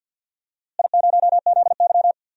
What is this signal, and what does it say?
Morse code, heard as a keyed tone, about 46 wpm, 700 Hz. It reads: I0ZY
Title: DXCC-100 Call Signs - 4 Characters - Repeated 3x - 50wpm